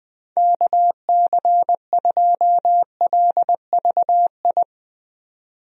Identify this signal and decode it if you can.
Morse code, heard as a keyed tone, about 20 wpm, 700 Hz. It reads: KC2LVI
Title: USA Call Signs - 6 Characters - 20wpm